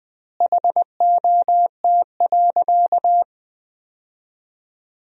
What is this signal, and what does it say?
Morse code, heard as a keyed tone, about 19 wpm, 700 Hz. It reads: HOT.
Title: 4x Word-Spacing - Sentences from Top 500 Words - 20wpm